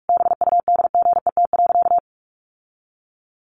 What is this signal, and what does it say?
Morse code, heard as a keyed tone, about 44 wpm, 700 Hz. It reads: BUDGET.
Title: Sentences from Top 1200 Words 45wpm